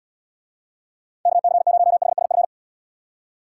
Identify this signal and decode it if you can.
Morse code, heard as a keyed tone, about 73 wpm, 700 Hz. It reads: KC0HSF